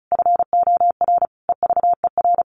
Morse code, heard as a keyed tone, about 34 wpm, 700 Hz. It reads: FOR EVER
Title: Sentences from Top 400 Words 35wpm